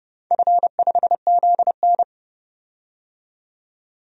Morse code, heard as a keyed tone, about 30 wpm, 700 Hz. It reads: F5ZN
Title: DXCC-100 Call Signs - 4 Characters - 30wpm